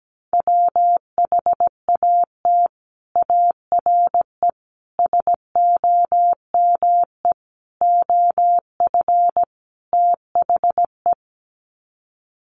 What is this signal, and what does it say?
Morse code, heard as a keyed tone, about 17 wpm, 700 Hz. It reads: WHAT ARE SOME OF THE
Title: Sets of 5 Words Encore 17wpm